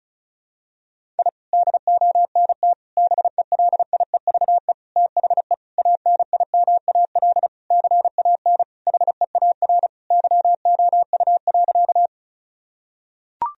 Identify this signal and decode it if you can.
Morse code, heard as a keyed tone, about 34 wpm, 700 Hz. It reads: I DONT BELIEVE THE ANIMAL CAN HEAR YOU.
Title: Sentences from Top 1000 Words - Review Part 1 - 35wpm